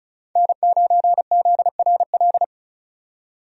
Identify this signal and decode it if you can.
Morse code, heard as a keyed tone, about 36 wpm, 700 Hz. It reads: N9ZRL
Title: USA Call Signs - 5 Characters - 35wpm